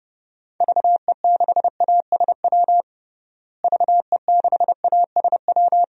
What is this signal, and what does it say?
Morse code, heard as a keyed tone, about 31 wpm, 700 Hz. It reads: VE6ASW VE6ASW
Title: DXCC-100 Call Signs - 6 Characters - Repeated 2x - 30wpm